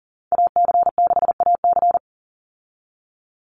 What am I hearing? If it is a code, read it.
Morse code, heard as a keyed tone, about 40 wpm, 700 Hz. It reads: AC6AC
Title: Call Signs - 40wpm